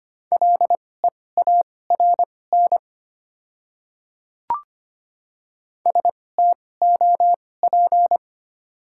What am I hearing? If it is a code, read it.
Morse code, heard as a keyed tone, about 25 wpm, spaced as for 18 wpm, 690 Hz. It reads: LEARN STOP
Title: 2x Character Spacing - Top 500 Words - 25wpm